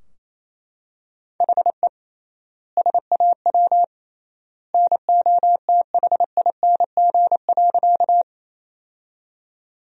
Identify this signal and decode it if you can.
Morse code, heard as a keyed tone, about 28 wpm, 700 Hz. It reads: HE SAW NOTHING.
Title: 3x Word-Spacing - Sentences from Top 500 Words - 28wpm